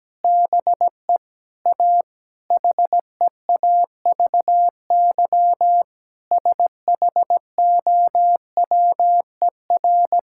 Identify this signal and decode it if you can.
Morse code, heard as a keyed tone, about 17 wpm, 700 Hz. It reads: BE A HEAVY SHOWER
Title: Sentences from Top 1600 Words 17wpm